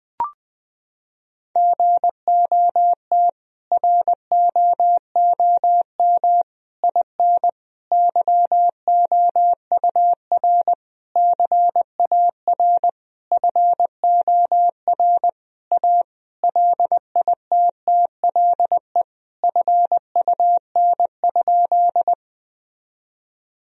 Morse code, heard as a keyed tone, about 20 wpm, 700 Hz. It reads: GOT ROOM IN YOUR CAR FOR A LITTLE FUN?
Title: Sentences from Top 1000 Words - Review Part 1 - 20wpm